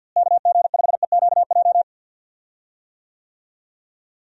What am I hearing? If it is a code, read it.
Morse code, heard as a keyed tone, about 52 wpm, 700 Hz. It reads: KG5EQJ
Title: USA Call Signs - 6 Characters - Repeated 3x - 50wpm